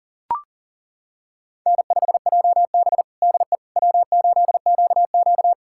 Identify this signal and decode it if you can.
Morse code, heard as a keyed tone, about 37 wpm, 700 Hz. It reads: N5JB DE W8QQ
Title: USA Callsign DE Callsign - 4 Characters - 40wpm